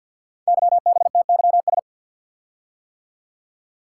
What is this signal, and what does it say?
Morse code, heard as a keyed tone, about 48 wpm, 700 Hz. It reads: YBTYS
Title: Code Groups - 5 Letters - 50wpm